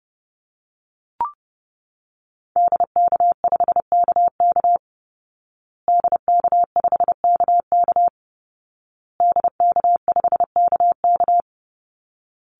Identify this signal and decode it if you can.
Morse code, heard as a keyed tone, about 30 wpm, 700 Hz. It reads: DK5KK DK5KK DK5KK
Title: Call Signs Repeated 2x - 30wpm